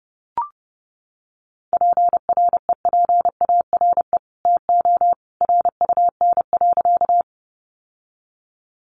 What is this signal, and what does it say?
Morse code, heard as a keyed tone, about 29 wpm, 700 Hz. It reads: PREPARE TO RUN.
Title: Sentences from Top 900 Words 30wpm